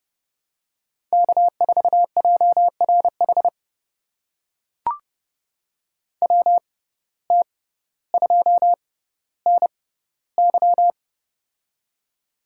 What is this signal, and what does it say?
Morse code, heard as a keyed tone, about 30 wpm, spaced as for 15 wpm, 700 Hz. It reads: K4JRH WT2NY
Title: USA Call Signs - 5 Characters - 6x Character Spacing - 30wpm